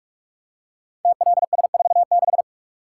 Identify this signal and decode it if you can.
Morse code, heard as a keyed tone, about 44 wpm, 700 Hz. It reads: TLS46
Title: Code Groups - 5 Letter:Numbers - 45wpm